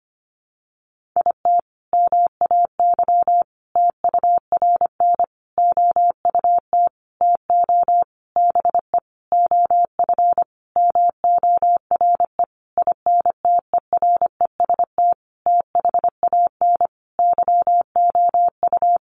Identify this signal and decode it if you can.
Morse code, heard as a keyed tone, about 25 wpm, 700 Hz. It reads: IT MAY TURN OUT TO BE OF MORE INTEREST THAN YOU
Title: Sentences from Top 400 Words 25wpm